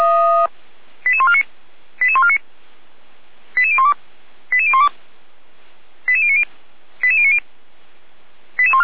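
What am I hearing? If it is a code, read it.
Five-tone selective call (ZVEI), heard as quick runs of five tones, tones of about 70 ms. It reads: 88149 88149 88012 88012 88090 88090
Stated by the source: Five-tone ZVEI selective call received on SDR.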